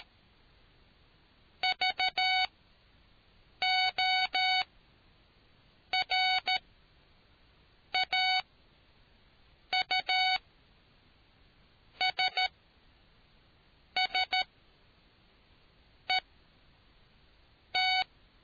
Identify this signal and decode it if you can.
Morse code, heard as a keyed tone, about 13 wpm, spaced as for 5 wpm, 750 Hz. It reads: VORAUSSET